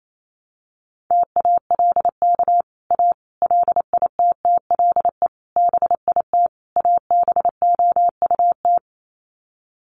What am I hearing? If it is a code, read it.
Morse code, heard as a keyed tone, about 28 wpm, 700 Hz. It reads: TALK A LITTLE BIT ABOUT
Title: Sets of 5 Words Encore 28wpm